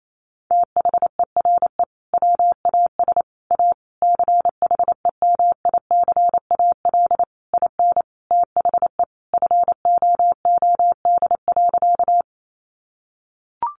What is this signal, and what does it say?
Morse code, heard as a keyed tone, about 28 wpm, 700 Hz. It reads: THERE WAS A CHEMICAL IN THE FOOD.
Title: Sentences from Top 1200 Words 28wpm